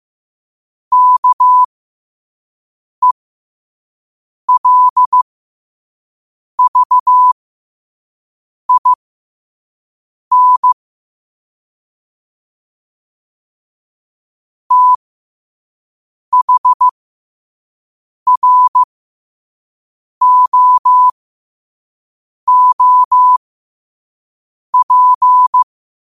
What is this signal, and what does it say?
Morse code, heard as a keyed tone, about 15 wpm, spaced as for 5 wpm, 1000 Hz. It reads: KELVIN THROOP
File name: qotdfiles / 5wpm